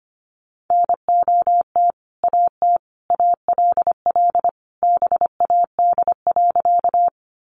Morse code, heard as a keyed tone, about 25 wpm, 700 Hz. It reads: NOT AT ALL BAD.
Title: Sentences from Top 700 Words 25wpm